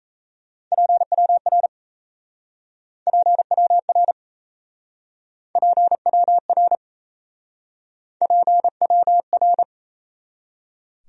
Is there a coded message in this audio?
Yes: Morse code from 0.7 to 9.6 s, heard as a keyed tone, about 37 wpm, 700 Hz. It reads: PWR PWR PWR PWR